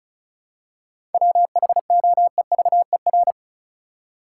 Morse code, heard as a keyed tone, about 34 wpm, 700 Hz. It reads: WHOEVER
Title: Seven Letter Words - 35wpm